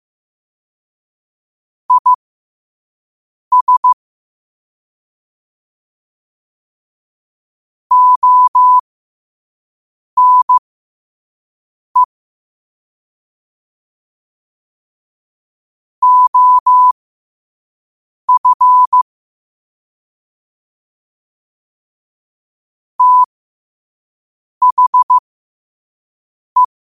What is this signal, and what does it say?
Morse code, heard as a keyed tone, about 15 wpm, spaced as for 5 wpm, 1000 Hz. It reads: IS ONE OF THE
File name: qotdfiles / 5wpm